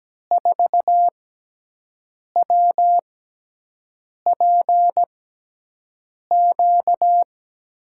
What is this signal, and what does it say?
Morse code, heard as a keyed tone, about 17 wpm, spaced as for 6 wpm, 700 Hz. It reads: 4WPQ
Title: USA Call Signs - 5 Characters - 6x Character Spacing - 17wpm